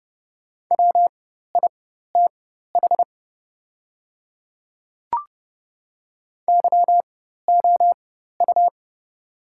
Morse code, heard as a keyed tone, about 31 wpm, spaced as for 14 wpm, 700 Hz. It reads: WITH YOU
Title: 4x Character Spacing - Top 200 Words - 30wpm